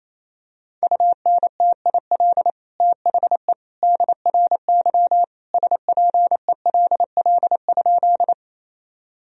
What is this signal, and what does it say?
Morse code, heard as a keyed tone, about 28 wpm, 700 Hz. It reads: UNTIL THE DRY SPELL?
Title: Sentences from Top 1000 Words - Review Part 3 - 28wpm